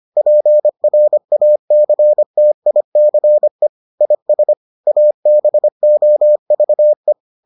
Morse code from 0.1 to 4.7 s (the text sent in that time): PRACTICE IS